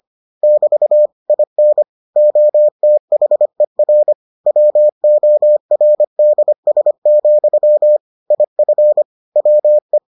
Morse code, heard as a keyed tone, about 25 wpm, 600 Hz. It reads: <BT> IN OTHER WORDS, IF WE